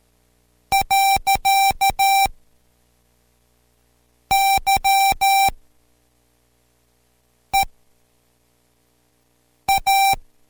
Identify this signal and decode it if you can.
Morse code, heard as a keyed tone, about 13 wpm, spaced as for 4 wpm, 780 Hz. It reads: . YEA